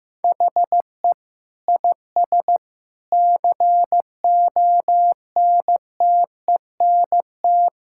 Morse code, heard as a keyed tone, about 16 wpm, 700 Hz. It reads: HE IS CONTENT
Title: Sentences from Top 1000 Words - Review Part 6 - 15wpm